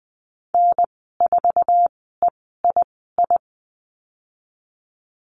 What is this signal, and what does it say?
Morse code, heard as a keyed tone, about 20 wpm, spaced as for 15 wpm, 700 Hz. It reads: N4EII
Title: 048 Lesson 9g - Number 4 - Call Signs 20wpm